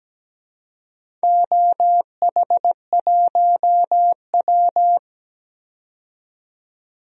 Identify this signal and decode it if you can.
Morse code, heard as a keyed tone, about 17 wpm, 700 Hz. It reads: OH1W
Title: DXCC-100 Call Signs - 4 Characters - 17wpm